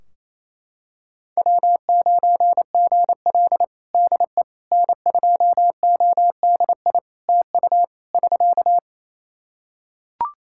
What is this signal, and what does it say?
Morse code, heard as a keyed tone, about 28 wpm, 700 Hz. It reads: W9GL DE N2ODI TU <SK>
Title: Practice QSO - Exchange 4 - Ending - 28wpm